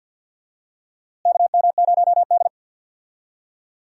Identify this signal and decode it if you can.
Morse code, heard as a keyed tone, about 52 wpm, 700 Hz. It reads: KM0D
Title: USA Call Signs - 4 Characters - Repeated 3x - 50wpm